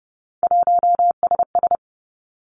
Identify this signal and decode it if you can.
Morse code, heard as a keyed tone, about 30 wpm, 700 Hz. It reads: 1SS
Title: Three Challenging Letter-Number Combinations 30wpm